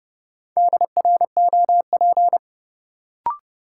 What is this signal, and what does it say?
Morse code, heard as a keyed tone, about 30 wpm, 700 Hz. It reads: DROP
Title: Sentences from Top 1400 Words - New Word Drill - 30wpm